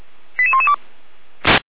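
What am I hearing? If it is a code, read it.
Five-tone selective call (ZVEI), heard as quick runs of five tones, tones of about 70 ms. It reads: 88112
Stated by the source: ZVEI five-tone selective call received by SDR.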